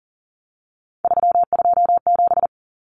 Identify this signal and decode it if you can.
Morse code, heard as a keyed tone, about 40 wpm, 700 Hz. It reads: 327
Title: Common Cut Numbers - 3-Digit 40wpm